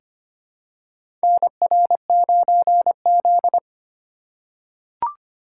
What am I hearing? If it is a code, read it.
Morse code, heard as a keyed tone, about 25 wpm, 700 Hz. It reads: NR9Z
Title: USA Call Signs - 4 Characters - 4x Character Spacing - 25wpm